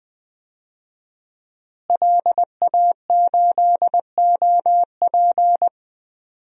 Morse code, heard as a keyed tone, about 20 wpm, 700 Hz. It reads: LA8OP
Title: DXCC-100 Call Signs - 5 Characters - Repeated 3x - 20wpm